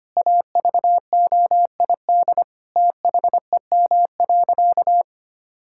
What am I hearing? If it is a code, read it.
Morse code, heard as a keyed tone, about 25 wpm, 700 Hz. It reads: AVOID THEM.
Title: Sentences from Top 1000 Words - Review Part 6 - 25wpm